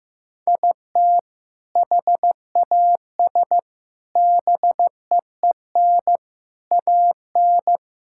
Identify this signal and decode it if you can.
Morse code, heard as a keyed tone, about 16 wpm, 700 Hz. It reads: IT HAS BEEN AN
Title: Sentences from Top 1000 Words - Review Part 5 - 15wpm